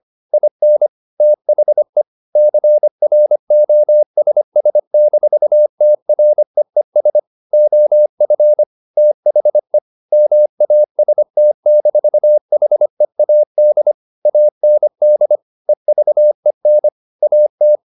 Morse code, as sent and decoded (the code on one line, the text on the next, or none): .. -. / - .... / -.-. .-. --- ... ... -....- - .-. . ... / --- ..-. / - .... / -- .- ... - -....- .... .- -.. / .- -. -.. / . ...- . -. / .- -
IN THE CROSS-TREES OF THE MAST-HEAD AND EVEN AT